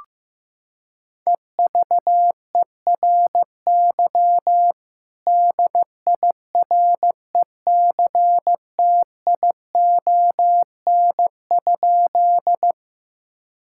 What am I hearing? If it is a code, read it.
Morse code, heard as a keyed tone, about 15 wpm, 700 Hz. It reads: EVERY DIRECTION?
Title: Sentences from Top 1000 Words - Review Part 5 - 15wpm